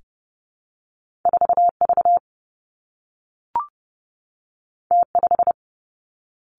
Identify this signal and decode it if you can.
Morse code, heard as a keyed tone, about 30 wpm, 700 Hz. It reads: 4 V T 5
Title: Two Challenging Letter-Number Combinations 30wpm